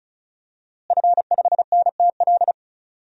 Morse code, heard as a keyed tone, about 34 wpm, 700 Hz. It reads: F5NTL
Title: DXCC-100 Call Signs - 5 Characters - Repeated 2x - 35wpm